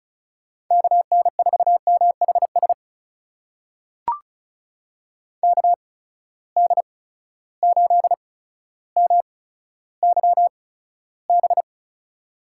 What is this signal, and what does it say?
Morse code, heard as a keyed tone, about 36 wpm, spaced as for 15 wpm, 700 Hz. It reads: KN4MHS KD8MYB
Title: USA Call Signs - 6 Characters - 8x Character Spacing - 35wpm